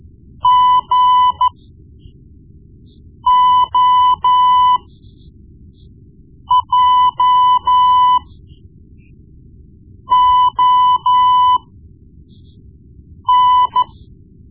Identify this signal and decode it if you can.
Morse code, heard as a keyed tone, about 13 wpm, spaced as for 4 wpm, 1000 Hz. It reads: GOJON
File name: Morse Lessons / Morse Lessons / Tape-1B